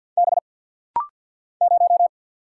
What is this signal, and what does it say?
Morse code, heard as a keyed tone, about 50 wpm, 700 Hz. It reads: B 0
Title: Single Letter-Number Alt - Rapid-Fire 50wpm